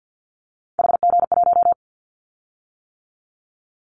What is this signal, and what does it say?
Morse code, heard as a keyed tone, about 49 wpm, 700 Hz. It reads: HG1
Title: Callsign Prefixes for Top 25 DXCC - 50wpm